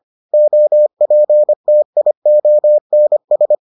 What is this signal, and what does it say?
Morse code, heard as a keyed tone, about 25 wpm, 600 Hz. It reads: OPTIONS